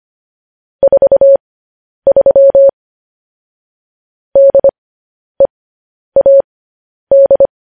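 Morse code, heard as a keyed tone, about 25 wpm, spaced as for 10 wpm, 550 Hz. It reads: 43 DEAD
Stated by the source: Morse news headlines at 10.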